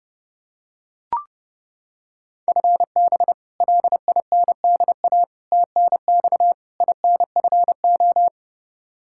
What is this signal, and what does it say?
Morse code, heard as a keyed tone, about 30 wpm, 700 Hz. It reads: FB LINDA TNX INFO
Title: Practice QSO - Exchange 3 - More Info - 30wpm